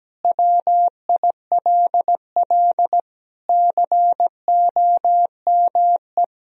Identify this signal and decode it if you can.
Morse code, heard as a keyed tone, about 17 wpm, 700 Hz. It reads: WILL COME
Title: Sentences from Top 1000 Words - Review Part 1 - 17wpm